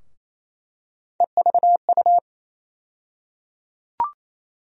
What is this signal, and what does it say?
Morse code, heard as a keyed tone, about 28 wpm, 700 Hz. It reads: EVU
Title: USA Call Sign Postfixes - 3 Letters 28wpm